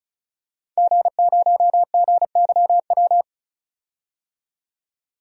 Morse code, heard as a keyed tone, about 34 wpm, 700 Hz. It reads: G0GYW
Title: DXCC-100 Call Signs - 5 Characters - 35wpm